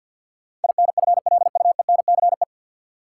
Morse code, heard as a keyed tone, about 53 wpm, 700 Hz. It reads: INFLUENCE
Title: Sentences from Top 1100 Words - New Word Drill Alt - 50wpm